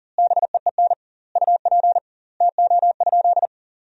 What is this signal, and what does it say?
Morse code, heard as a keyed tone, about 40 wpm, 700 Hz. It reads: BEEN UP TO?
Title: 2x Word-Spacing - Sentences from Top 100 Words - 40wpm